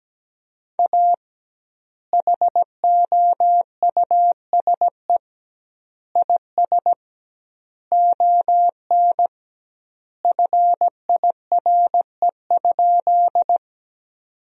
Morse code, heard as a keyed tone, about 17 wpm, 700 Hz. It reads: A HOUSE IS ON FIRE?
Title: 2x Word-Spacing - Sentences from Top 400 Words - 17wpm